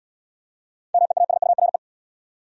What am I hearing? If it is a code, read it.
Morse code, heard as a keyed tone, about 74 wpm, 700 Hz. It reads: MEASURE